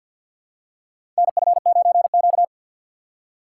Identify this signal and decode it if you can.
Morse code, heard as a keyed tone, about 47 wpm, 700 Hz. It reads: NF9Q
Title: USA Call Signs - 4x Character Spacing - 50wpm